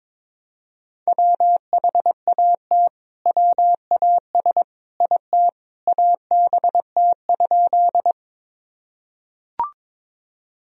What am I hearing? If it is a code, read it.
Morse code, heard as a keyed tone, about 22 wpm, 700 Hz. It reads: WHAT WAS IT ABT?
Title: Sentences from Top 100 Words and Abbreviations 22wpm